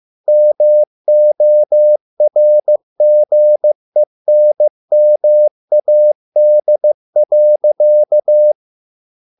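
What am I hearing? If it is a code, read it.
Morse code, heard as a keyed tone, about 15 wpm, 600 Hz. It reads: MORGENMAD.
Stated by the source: Morse code at a slow 15 wpm from CW med Gnister.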